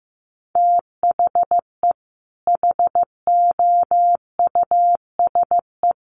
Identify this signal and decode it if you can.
Morse code, heard as a keyed tone, about 15 wpm, 700 Hz. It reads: THE HOUSE
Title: Sets of 2 Words 15wpm